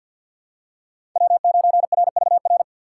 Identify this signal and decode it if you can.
Morse code, heard as a keyed tone, about 47 wpm, 700 Hz. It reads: W9RVR
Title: USA Call Signs - 6x Character Spacing - 50wpm